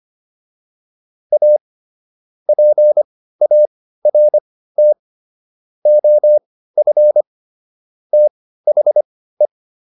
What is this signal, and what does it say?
Morse code, heard as a keyed tone, about 25 wpm, spaced as for 15 wpm, 600 Hz. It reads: A PART OF THE